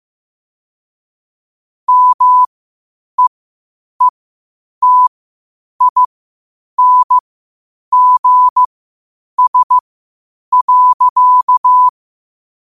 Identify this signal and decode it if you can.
Morse code, heard as a keyed tone, about 15 wpm, spaced as for 8 wpm, 1000 Hz. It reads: MEETINGS.
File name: qotdfiles / 8wpm